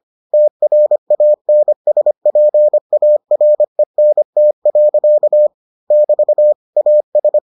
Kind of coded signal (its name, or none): Morse code